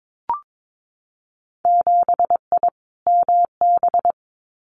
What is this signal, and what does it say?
Morse code, heard as a keyed tone, about 22 wpm, 700 Hz. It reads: 7I MB
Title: Winter Field Day Exchanges Encore 22wpm